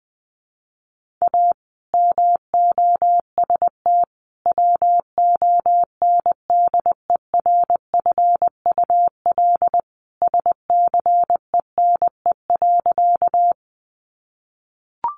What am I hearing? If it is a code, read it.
Morse code, heard as a keyed tone, about 20 wpm, 700 Hz. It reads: A MOST WONDERFUL SCENE.
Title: Sentences from Top 1100 Words 20wpm